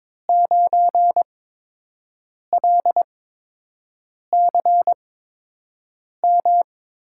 Morse code, heard as a keyed tone, about 22 wpm, spaced as for 6 wpm, 700 Hz. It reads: 9LCM
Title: USA Call Signs - 5 Characters - 8x Character Spacing - 22wpm